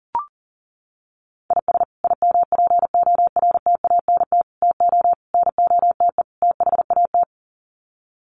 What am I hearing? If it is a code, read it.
Morse code, heard as a keyed tone, about 37 wpm, 700 Hz. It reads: IS IMPORTANT TO NOTE THAT
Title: Sets of 5 Words 40wpm